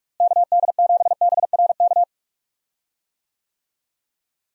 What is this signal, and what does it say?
Morse code, heard as a keyed tone, about 41 wpm, 700 Hz. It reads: KD7BRK
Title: USA Call Signs - 6 Characters - Repeated 3x - 45wpm